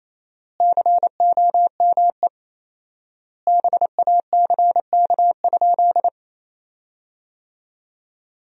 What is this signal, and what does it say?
Morse code, heard as a keyed tone, about 28 wpm, 700 Hz. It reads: COME BACK?
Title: 4x Word-Spacing - Sentences from Top 300 Words - 28wpm